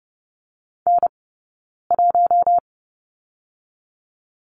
Morse code, heard as a keyed tone, about 30 wpm, 700 Hz. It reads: N 1
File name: 092 Lesson 16d - Letter c - Two Characters 30wpm